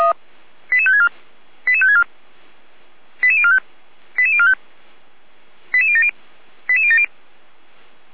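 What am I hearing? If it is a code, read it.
Five-tone selective call (ZVEI), heard as quick runs of five tones, tones of about 70 ms. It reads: 88564 88564 88045 88045 88070 88070
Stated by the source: ZVEI five-tone call received by SDR.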